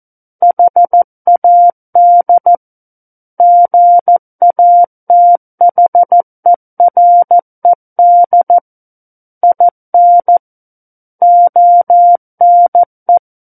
Morse code, heard as a keyed tone, about 14 wpm, 700 Hz. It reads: HAD GATHERED IN ONE
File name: Book One / 05-Chapter 5 - 14 WPM